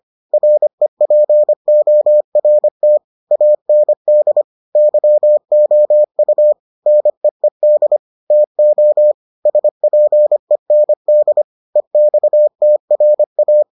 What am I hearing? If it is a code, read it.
Morse code, heard as a keyed tone, about 25 wpm, 600 Hz. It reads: REPORT AND YOU NEED TO SPEND EXTRA